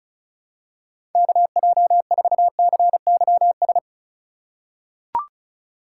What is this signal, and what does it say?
Morse code, heard as a keyed tone, about 34 wpm, 700 Hz. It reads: KJ4CYS
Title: USA Call Signs - 6 Characters - 4x Character Spacing - 35wpm